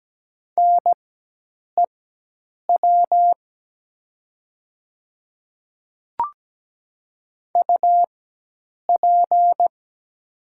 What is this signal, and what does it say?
Morse code, heard as a keyed tone, about 17 wpm, spaced as for 8 wpm, 700 Hz. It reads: NEW UP
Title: 4x Character Spacing - Top 500 Words - 17wpm